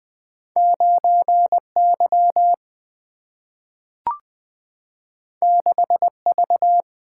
Morse code, heard as a keyed tone, about 20 wpm, 700 Hz. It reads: 9Y 6V
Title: Single-Letter Single-Number 20wpm